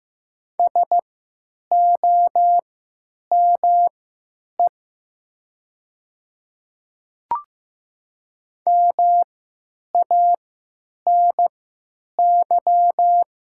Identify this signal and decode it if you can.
Morse code, heard as a keyed tone, about 15 wpm, spaced as for 9 wpm, 700 Hz. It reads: SOME MANY
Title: 3x Character Spacing - Top 200 Words - 15wpm